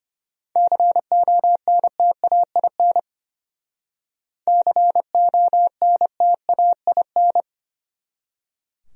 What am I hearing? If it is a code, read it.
Morse code, heard as a keyed tone, about 33 wpm, 700 Hz. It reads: CONTAIN CONTAIN